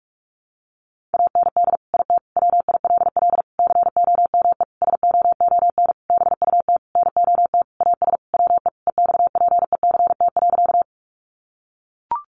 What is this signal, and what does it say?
Morse code, heard as a keyed tone, about 44 wpm, 700 Hz. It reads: AND IT WILL COME SOON BUT NOT AS WE EXPECT.
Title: Sentences from Top 1100 Words 45wpm